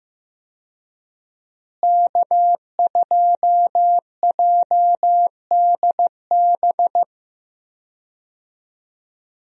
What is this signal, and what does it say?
Morse code, heard as a keyed tone, about 16 wpm, 700 Hz. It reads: K2JDB
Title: USA Call Signs - 5 Characters - Repeated 3x - 15wpm